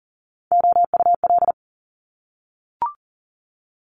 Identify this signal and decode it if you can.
Morse code, heard as a keyed tone, about 42 wpm, 700 Hz. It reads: OUL
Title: Top Three Letter Combinations in Words 40wpm